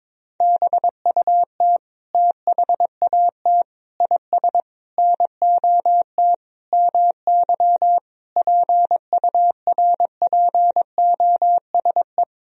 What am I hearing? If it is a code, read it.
Morse code, heard as a keyed tone, about 22 wpm, 700 Hz. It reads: BUT THAT IS NOT MY PURPOSE
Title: Sentences from Top 1000 Words - Review Part 4 - 22wpm